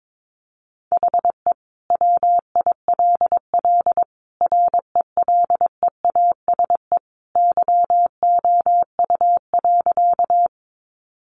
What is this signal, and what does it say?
Morse code, heard as a keyed tone, about 22 wpm, 700 Hz. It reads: HE WILL RELEASE YOU.
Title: Sentences from Top 1100 Words 22wpm